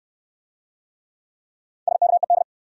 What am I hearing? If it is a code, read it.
Morse code, heard as a keyed tone, about 61 wpm, 700 Hz. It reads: SKED